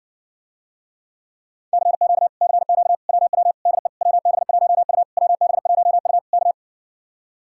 Morse code, heard as a keyed tone, about 60 wpm, 700 Hz. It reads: CQ CQ WW DE WB1U WB1U K